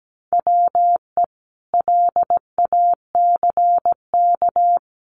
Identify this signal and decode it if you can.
Morse code, heard as a keyed tone, about 17 wpm, 700 Hz. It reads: WE LACK
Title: Sentences from Top 1400 Words 17wpm